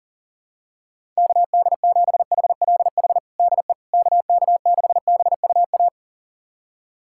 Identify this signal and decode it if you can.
Morse code, heard as a keyed tone, about 40 wpm, 700 Hz. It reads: KD7HLH DE KK6BUA
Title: USA Callsign DE Callsign - 6 Characters - 40wpm